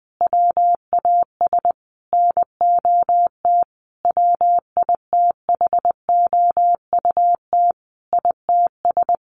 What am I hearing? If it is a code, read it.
Morse code, heard as a keyed tone, about 19 wpm, 700 Hz. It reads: WAS NOT WITHOUT ITS
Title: Sentences from Top 1500 Words 20wpm